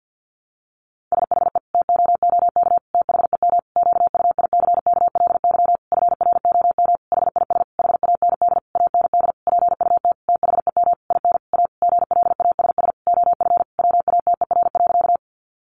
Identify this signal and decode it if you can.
Morse code, heard as a keyed tone, about 50 wpm, 700 Hz. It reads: SHE TOOK THEM QUICKLY FROM HIS HAND AND PUT THEM IN A GLASS OF WATER.
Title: Sentences from Top 1300 Words 50wpm